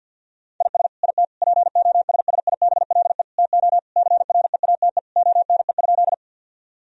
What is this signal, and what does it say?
Morse code, heard as a keyed tone, about 50 wpm, 700 Hz. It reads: IS IT POSSIBLE TO CREATE ONE?
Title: Sentences from Top 1000 Words - Review Part 3 - 50wpm